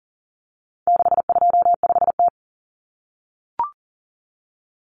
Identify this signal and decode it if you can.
Morse code, heard as a keyed tone, about 40 wpm, 700 Hz. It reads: 625T
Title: Common Cut Numbers - 4-Digit 40wpm